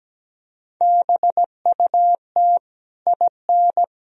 Morse code, heard as a keyed tone, about 17 wpm, 700 Hz. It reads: BUT IN
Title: Sentences from Top 1000 Words - Review Part 4 - 17wpm